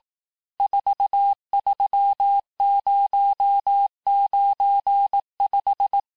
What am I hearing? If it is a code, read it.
Morse code, heard as a keyed tone, about 18 wpm, 800 Hz. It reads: 43095
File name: cours-du-lundi files / cours-1594--teste-chiffre-vitesse-17mpm